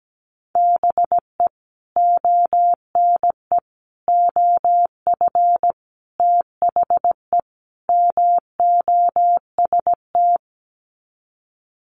Morse code, heard as a keyed tone, about 17 wpm, 700 Hz. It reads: BE ONE OF THE MOST
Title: Sets of 5 Words Encore 17wpm